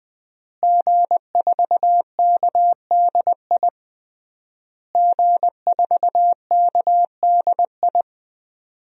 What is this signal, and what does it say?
Morse code, heard as a keyed tone, about 20 wpm, 700 Hz. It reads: G4KDI G4KDI
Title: DXCC-100 Call Signs - 5 Characters - Repeated 3x - 20wpm